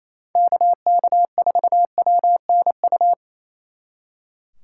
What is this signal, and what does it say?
Morse code, heard as a keyed tone, about 37 wpm, 700 Hz. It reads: KK4WNU